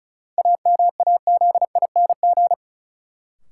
Morse code, heard as a keyed tone, about 36 wpm, 700 Hz. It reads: AMAZING
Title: Sentences from Top 1500 Words - New Word Drill Alt - 35wpm